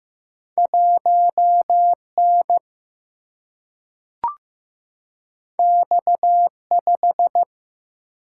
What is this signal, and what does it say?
Morse code, heard as a keyed tone, about 15 wpm, 700 Hz. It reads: 1 N X 5
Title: Single-Letter Single-Number 15wpm